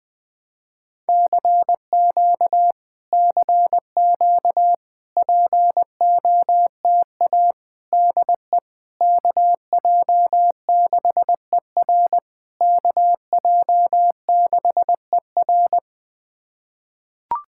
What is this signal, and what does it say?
Morse code, heard as a keyed tone, about 20 wpm, 700 Hz. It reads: CQ CQ POTA DE KJ6ER KJ6ER
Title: POTA QSO Practice - 20wpm